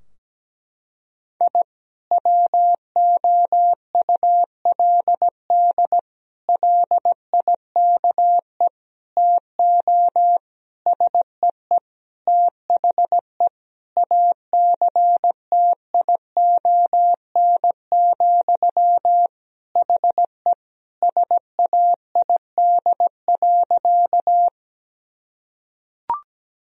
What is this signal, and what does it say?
Morse code, heard as a keyed tone, about 17 wpm, 700 Hz. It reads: I WOULD LIKE TO SEE THE ACTION, HE SAID.
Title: Sentences from Top 1000 Words - Review Part 4 - 17wpm